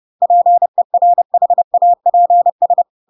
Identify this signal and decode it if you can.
Morse code, heard as a keyed tone, about 30 wpm, 700 Hz. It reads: PERHAPS
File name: Pinocchio / Pinocchio ch4 30 30 ik1hge